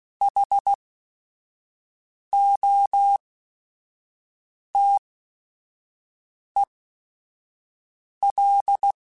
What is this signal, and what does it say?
Morse code, heard as a keyed tone, about 16 wpm, spaced as for 5 wpm, 800 Hz. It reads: HOTEL